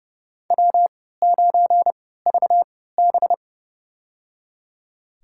Morse code, heard as a keyed tone, about 30 wpm, spaced as for 17 wpm, 700 Hz. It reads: W9VB
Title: USA Call Signs - 4 Characters - 3x Character Spacing - 30wpm